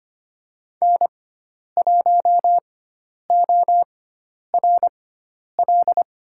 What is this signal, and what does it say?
Morse code, heard as a keyed tone, about 25 wpm, spaced as for 10 wpm, 700 Hz. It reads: N1ORL
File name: Lesson 16g - Letter c - Call Signs 25wpm